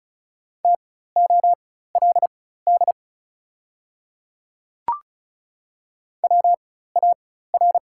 Morse code, heard as a keyed tone, about 36 wpm, spaced as for 16 wpm, 690 Hz. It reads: TOLD WAR
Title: 4x Character Spacing - Top 500 Words - 35wpm